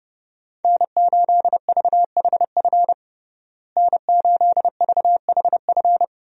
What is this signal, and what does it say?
Morse code, heard as a keyed tone, about 31 wpm, 700 Hz. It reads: N8VHF N8VHF
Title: USA Call Signs - 5 Characters - Repeated 2x - 30wpm